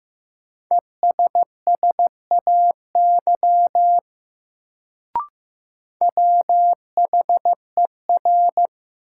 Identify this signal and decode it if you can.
Morse code, heard as a keyed tone, about 15 wpm, 700 Hz. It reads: ESSAY WHER
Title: Sentences from Top 1500 Words - New Word Drill - 15wpm